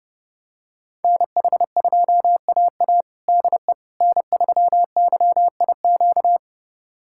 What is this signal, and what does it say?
Morse code, heard as a keyed tone, about 32 wpm, 700 Hz. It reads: NH2AA DE N3YIQ
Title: USA Callsign DE Callsign - 5 Characters - 30wpm